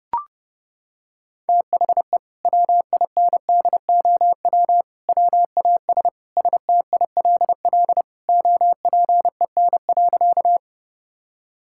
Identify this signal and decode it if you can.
Morse code, heard as a keyed tone, about 32 wpm, 700 Hz. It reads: THE WINDOW WAS STILL OPEN.
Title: Sentences from Top 1600 Words 30wpm